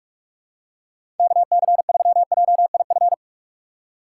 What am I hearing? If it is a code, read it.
Morse code, heard as a keyed tone, about 47 wpm, 700 Hz. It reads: KC3JIF
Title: USA Call Signs - 6 Characters - 3x Character Spacing - 45wpm